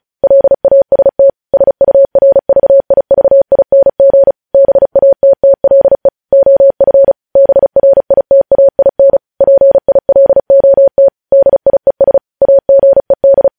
Morse code, heard as a keyed tone, about 36 wpm, 550 Hz. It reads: LAST SURVIVING BATTLE OF BRITAIN PILOT DIES AGED
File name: podcast / News headlines-1x-35x35